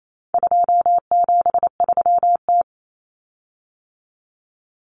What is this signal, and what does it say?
Morse code, heard as a keyed tone, about 28 wpm, 700 Hz. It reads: 273T
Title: Common Cut Numbers - 4-Digit 28wpm